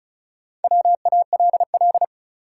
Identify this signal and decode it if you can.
Morse code, heard as a keyed tone, about 34 wpm, 700 Hz. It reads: WALL
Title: Sentences from Top 1100 Words - New Word Drill Alt - 35wpm